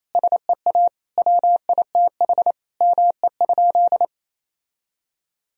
Morse code, heard as a keyed tone, about 27 wpm, 700 Hz. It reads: SEA WITH ME?
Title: Sentences from Top 1000 Words - Review Part 2 - 28wpm